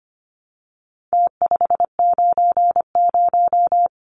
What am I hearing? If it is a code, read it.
Morse code, heard as a keyed tone, about 25 wpm, 700 Hz. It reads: T590
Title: Numbers - Contest Style - 4-Digit 25wpm